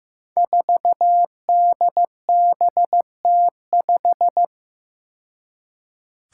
Morse code, heard as a keyed tone, about 15 wpm, 700 Hz. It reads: 4DBT5
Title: Five Challenging Letter-Number Combinations 15wpm